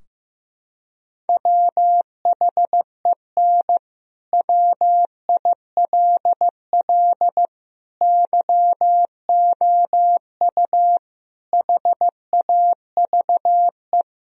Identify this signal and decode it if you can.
Morse code, heard as a keyed tone, about 15 wpm, 700 Hz. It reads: WHEN WILL YOU HAVE
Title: Sentences from Top 1000 Words - Review Part 3 - 15wpm